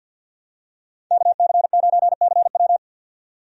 Morse code, heard as a keyed tone, about 52 wpm, 700 Hz. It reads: KC9CW
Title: USA Call Signs - 5 Characters - 50wpm